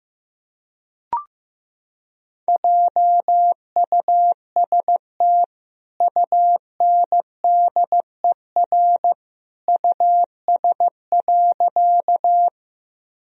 Morse code, heard as a keyed tone, about 15 wpm, 700 Hz. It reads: JUST UNDER US.
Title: Sentences from Top 1000 Words - Review Part 1 - 15wpm